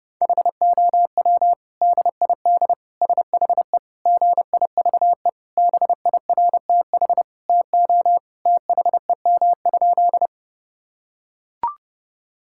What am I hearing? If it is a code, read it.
Morse code, heard as a keyed tone, about 32 wpm, 700 Hz. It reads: HOW DID SHE GIVE BIRTH TO THEM?
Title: Sentences from Top 1000 Words - Review Part 5 - 30wpm